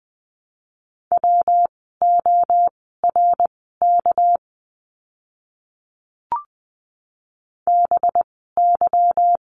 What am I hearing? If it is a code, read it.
Morse code, heard as a keyed tone, about 20 wpm, spaced as for 14 wpm, 700 Hz. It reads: WORK BY
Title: 199 Lesson 32f - Letter k - Words 20wpm